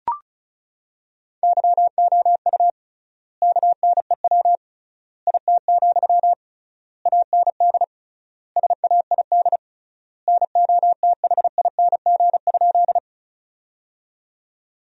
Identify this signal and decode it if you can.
Morse code, heard as a keyed tone, about 34 wpm, 700 Hz. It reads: YOU KNEW IT, AND SAID NOTHING?
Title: 3x Word-Spacing - Sentences from Top 500 Words - 35wpm